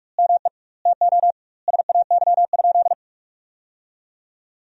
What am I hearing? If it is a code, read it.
Morse code, heard as a keyed tone, about 44 wpm, 700 Hz. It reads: ME TO SAY?
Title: 2x Word-Spacing - Sentences from Top 200 Words - 45wpm